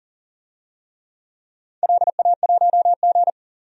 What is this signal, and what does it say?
Morse code, heard as a keyed tone, about 42 wpm, 700 Hz. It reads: LA1G
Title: DXCC-100 Call Signs - 4 Characters - 40wpm